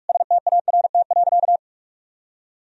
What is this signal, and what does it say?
Morse code, heard as a keyed tone, about 41 wpm, 700 Hz. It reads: START.
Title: Sentences from Top 1000 Words - Review Part 4 - 45wpm